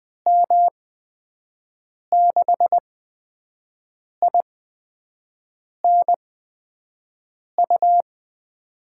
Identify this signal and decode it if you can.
Morse code, heard as a keyed tone, about 20 wpm, spaced as for 5 wpm, 700 Hz. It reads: M6INU
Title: USA Call Signs - 8x Character Spacing - 20wpm